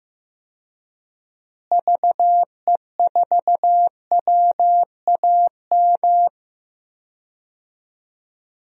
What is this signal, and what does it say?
Morse code, heard as a keyed tone, about 15 wpm, 700 Hz. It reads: VE4WAM
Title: DXCC-100 Call Signs - 6 Characters - Repeated 3x - 15wpm